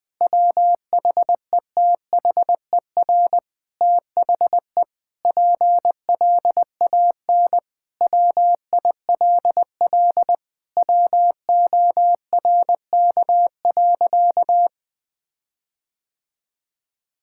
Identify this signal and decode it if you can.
Morse code, heard as a keyed tone, about 20 wpm, 700 Hz. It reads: WHETHER THE PLAN WILL WORK.
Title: Sentences with 6-Word Phrases - Spoken Beginning - 20wpm